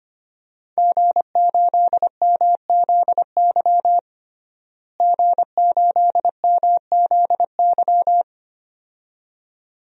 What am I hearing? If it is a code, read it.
Morse code, heard as a keyed tone, about 25 wpm, 700 Hz. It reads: G8MZY G8MZY
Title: DXCC-100 Call Signs - 5 Characters - Repeated 2x - 25wpm